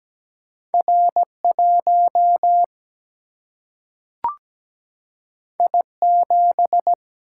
Morse code, heard as a keyed tone, about 17 wpm, 700 Hz. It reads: R1 I7
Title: Single-Letter Single-Number 17wpm